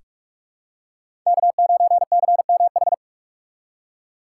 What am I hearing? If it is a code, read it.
Morse code, heard as a keyed tone, about 46 wpm, 700 Hz. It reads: K9CMH
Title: USA Call Signs - 5 Characters - 4x Character Spacing - 45wpm